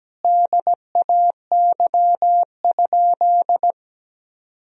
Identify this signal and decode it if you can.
Morse code, heard as a keyed tone, about 17 wpm, 700 Hz. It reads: DAY?
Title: Sentences from Top 1000 Words - Review Part 4 - 17wpm